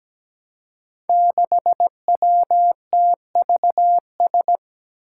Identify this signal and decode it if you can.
Morse code, heard as a keyed tone, about 17 wpm, 700 Hz. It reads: 6WTVS
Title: Five Challenging Letter-Number Combinations 17wpm